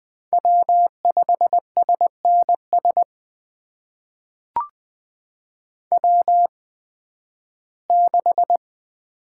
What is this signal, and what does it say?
Morse code, heard as a keyed tone, about 20 wpm, spaced as for 13 wpm, 700 Hz. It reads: W5SNS W6
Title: USA Call Signs - 5 Characters - 8x Character Spacing - 20wpm